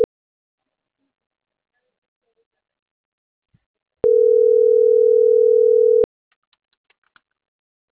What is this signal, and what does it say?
Ringback tone (North America), 440 + 480 Hz.